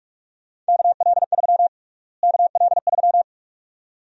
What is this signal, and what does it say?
Morse code, heard as a keyed tone, about 47 wpm, 700 Hz. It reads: KL3 KL3
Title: USA Call Sign Prefixes - Repeated 2x - 45wpm